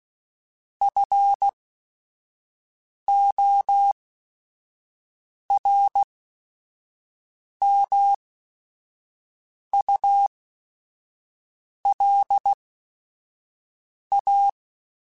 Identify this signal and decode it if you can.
Morse code, heard as a keyed tone, about 16 wpm, spaced as for 5 wpm, 800 Hz. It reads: FORMULA
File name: Technical Subjects / Technical Subjects-mp3s / XL @5